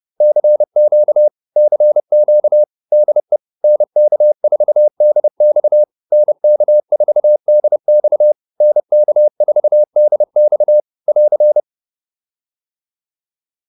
Morse code, heard as a keyed tone, about 30 wpm, 600 Hz. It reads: CQ CQ DE NK4DX NK4DX NK4DX <AR>